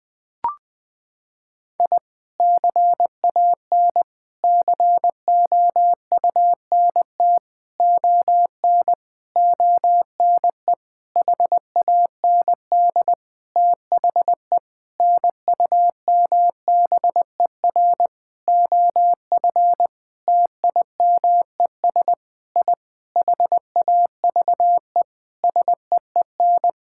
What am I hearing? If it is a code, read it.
Morse code, heard as a keyed tone, about 20 wpm, 700 Hz. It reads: I CAN COUNT ON ONE HAND THE NUMBER OF TIMES I HAVE SEEN
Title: Sentences with 8-Word Phrases - 20wpm